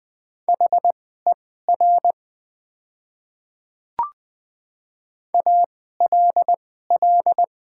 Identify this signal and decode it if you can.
Morse code, heard as a keyed tone, about 20 wpm, spaced as for 15 wpm, 700 Hz. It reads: HER ALL
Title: 2x Character Spacing - Top 100 Words - 20wpm